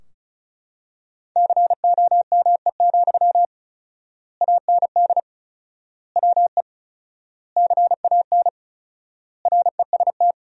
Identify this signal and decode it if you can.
Morse code, heard as a keyed tone, about 34 wpm, 700 Hz. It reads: COME, AND WE CAN REST
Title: 4x Word-Spacing - Sentences from Top 500 Words - 35wpm